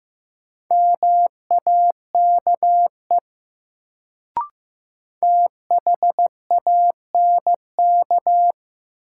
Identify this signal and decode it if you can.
Morse code, heard as a keyed tone, about 15 wpm, 700 Hz. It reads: MAKE THANK
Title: Sets of 2 Words - New Word Drill - 15wpm